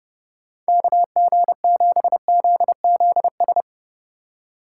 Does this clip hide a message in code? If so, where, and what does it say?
Yes: Morse code from 0.7 to 3.6 s, heard as a keyed tone, about 38 wpm, 700 Hz. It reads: KG7ZZS